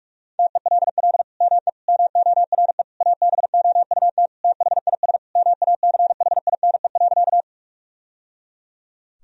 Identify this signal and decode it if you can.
Morse code, heard as a keyed tone, about 46 wpm, 700 Hz. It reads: TELL ME MORE ABOUT THIS MACHINE.
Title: Sentences from Top 1000 Words - Review Part 2 - 45wpm